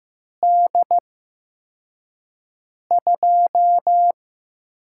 Morse code, heard as a keyed tone, about 15 wpm, spaced as for 4 wpm, 700 Hz. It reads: D2
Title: USA Call Signs - 4 Characters - 8x Character Spacing - 15wpm